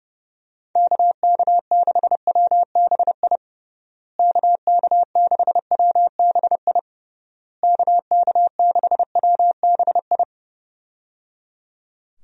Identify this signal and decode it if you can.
Morse code, heard as a keyed tone, about 30 wpm, 700 Hz. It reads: KK6WBI KK6WBI KK6WBI
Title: USA Call Signs - 6 Characters - Repeated 3x - 30wpm